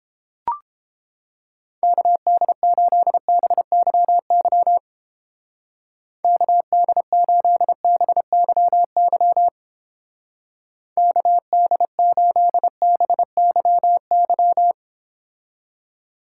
Morse code, heard as a keyed tone, about 29 wpm, 700 Hz. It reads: KD8BYY KD8BYY KD8BYY